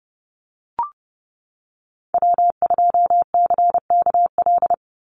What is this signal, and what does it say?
Morse code, heard as a keyed tone, about 30 wpm, 700 Hz. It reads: W2CKL
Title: NA SOTA Activator Callsigns 30wpm